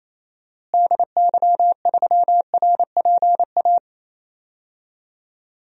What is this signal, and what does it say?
Morse code, heard as a keyed tone, about 28 wpm, 700 Hz. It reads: DY3RPA
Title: DXCC-100 Call Signs - 6 Characters - Repeated 2x - 28wpm